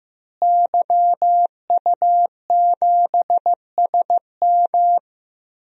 Morse code, heard as a keyed tone, about 14 wpm, 700 Hz. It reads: YU7SM
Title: DXCC-100 Call Signs - 5 Characters - Repeated 3x - 15wpm